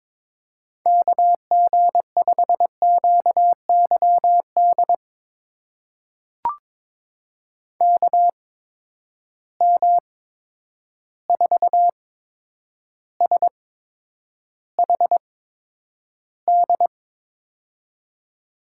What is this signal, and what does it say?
Morse code, heard as a keyed tone, about 22 wpm, spaced as for 9 wpm, 700 Hz. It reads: KG5QYD KM4SHD
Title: USA Call Signs - 6 Characters - 8x Character Spacing - 22wpm